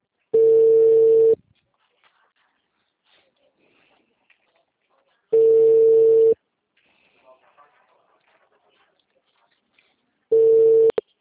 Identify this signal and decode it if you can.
Ringback tone (China, 450 Hz).
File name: out- -2002